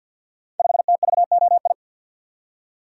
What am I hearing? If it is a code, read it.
Morse code, heard as a keyed tone, about 49 wpm, 700 Hz. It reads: 5TVOI
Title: Code Groups - 5 Letter:Numbers - 50wpm